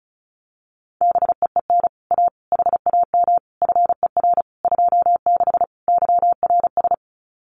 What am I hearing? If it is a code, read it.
Morse code, heard as a keyed tone, about 34 wpm, 700 Hz. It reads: BEEN A HAM FER 26 YRS
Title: Common QSO Phrases Encore 35wpm